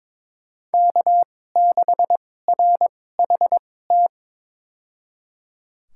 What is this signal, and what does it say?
Morse code, heard as a keyed tone, about 22 wpm, spaced as for 16 wpm, 700 Hz. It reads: K6RHT
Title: USA Call Signs - 5 Characters - 2x Character Spacing - 22wpm